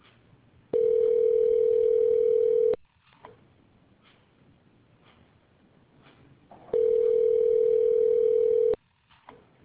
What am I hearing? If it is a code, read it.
Ringback tone (North America), 440 + 480 Hz.